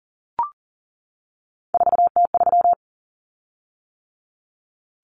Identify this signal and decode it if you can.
Morse code, heard as a keyed tone, about 40 wpm, 700 Hz. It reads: 4T3
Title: Common Cut Numbers - 3-Digit 40wpm